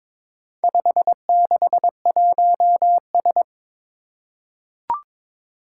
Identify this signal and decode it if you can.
Morse code, heard as a keyed tone, about 22 wpm, 700 Hz. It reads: 561S
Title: Four Challenging Letter-Number Combinations 22wpm